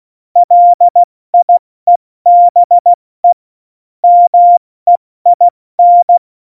Morse code, heard as a keyed tone, about 16 wpm, 700 Hz. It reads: LIEBE MEIN
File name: Morsekurs / Witze / Witz 05 Mai, 2020 70 F80